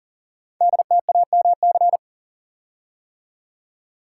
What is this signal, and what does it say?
Morse code, heard as a keyed tone, about 38 wpm, 700 Hz. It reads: DTAMC
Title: Code Groups - 5 Letters - 40wpm